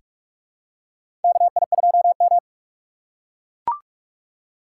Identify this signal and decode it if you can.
Morse code, heard as a keyed tone, about 41 wpm, 700 Hz. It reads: KI2M
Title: USA Call Signs - 4 Characters - 4x Character Spacing - 45wpm